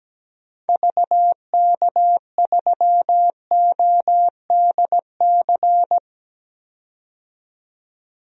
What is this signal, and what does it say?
Morse code, heard as a keyed tone, about 17 wpm, 700 Hz. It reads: VK3ODC
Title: DXCC-100 Call Signs - 6 Characters - 17wpm